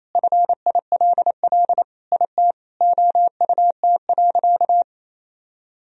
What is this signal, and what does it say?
Morse code, heard as a keyed tone, about 28 wpm, 700 Hz. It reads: FILL IT OUT.
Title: Sentences from Top 1000 Words - Review Part 3 - 28wpm